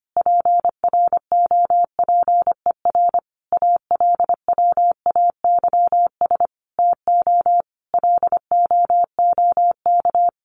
Morse code, heard as a keyed tone, about 25 wpm, 700 Hz. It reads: PROPER ALWAYS TO LOOK
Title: Sentences from Top 1500 Words 25wpm